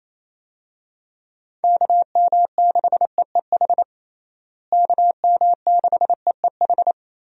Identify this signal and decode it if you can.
Morse code, heard as a keyed tone, about 28 wpm, 700 Hz. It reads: KM6EEH KM6EEH
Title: USA Call Signs - 6 Characters - Repeated 2x - 28wpm